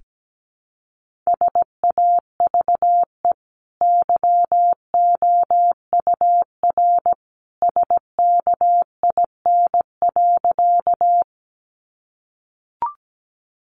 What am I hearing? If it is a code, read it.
Morse code, heard as a keyed tone, about 17 wpm, 700 Hz. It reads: SAVE YOUR SKIN.
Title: Sentences from Top 1300 Words 17wpm